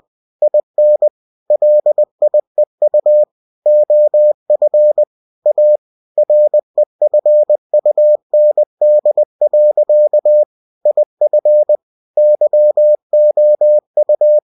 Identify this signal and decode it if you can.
Morse code, heard as a keyed tone, about 20 wpm, 600 Hz. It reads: IN LIEU OF A REFUND. IF YOU
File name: cw-books / winnie-the-pooh / winnie-20wpm-page-0119